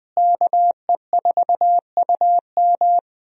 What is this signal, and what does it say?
Morse code, heard as a keyed tone, about 20 wpm, 700 Hz. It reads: KE4UM
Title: USA Call Signs - 5 Characters - Repeated 2x - 20wpm